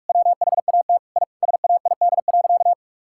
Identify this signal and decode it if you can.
Morse code, heard as a keyed tone, about 46 wpm, 700 Hz. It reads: WHAT I SAID.
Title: Sentences from Top 1000 Words - Review Part 6 - 45wpm